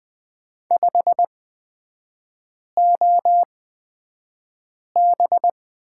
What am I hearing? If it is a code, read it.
Morse code, heard as a keyed tone, about 20 wpm, spaced as for 5 wpm, 700 Hz. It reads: 5OB
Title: 243 Lesson 39e - Letter z - Three Characters 20wpm